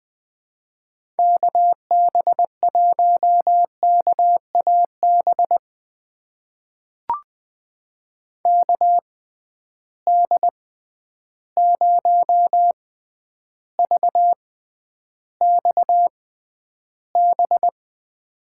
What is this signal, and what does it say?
Morse code, heard as a keyed tone, about 20 wpm, spaced as for 10 wpm, 700 Hz. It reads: KB1KAB KD0VXB
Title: USA Call Signs - 6 Characters - 6x Character Spacing - 20wpm